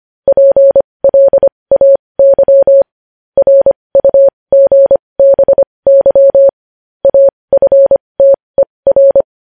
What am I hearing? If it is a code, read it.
Morse code, heard as a keyed tone, about 25 wpm, spaced as for 20 wpm, 550 Hz. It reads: PLAY RUGBY AFTER
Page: News Headlines in Morse Code at 20 WPM